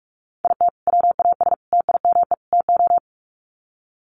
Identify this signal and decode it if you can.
Morse code, heard as a keyed tone, about 44 wpm, 700 Hz. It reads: IT WAS TIME TO
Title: Sets of 4 Words 45wpm